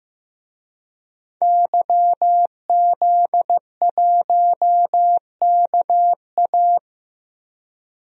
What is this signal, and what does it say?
Morse code, heard as a keyed tone, about 15 wpm, 700 Hz. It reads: YZ1KA
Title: DXCC-100 Call Signs - 5 Characters - 15wpm